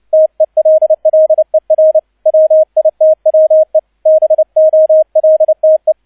Morse code, heard as a keyed tone, about 29 wpm, 630 Hz, with radio noise. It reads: TELLER WITWE BOLTE
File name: 002 Zweiter Streich 30wpm